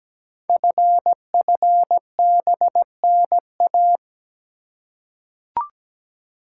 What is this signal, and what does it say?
Morse code, heard as a keyed tone, about 17 wpm, 700 Hz. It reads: FFBNA
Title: Code Groups - 5 Letters - 17wpm